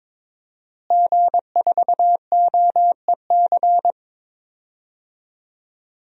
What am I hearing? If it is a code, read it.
Morse code, heard as a keyed tone, about 22 wpm, 700 Hz. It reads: G4OEC
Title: DXCC-100 Call Signs - 5 Characters - Repeated 2x - 22wpm